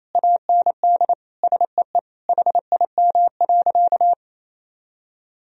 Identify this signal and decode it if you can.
Morse code, heard as a keyed tone, about 28 wpm, 700 Hz. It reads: AND SEE HIM.
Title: Sentences from Top 1000 Words - Review Part 3 - 28wpm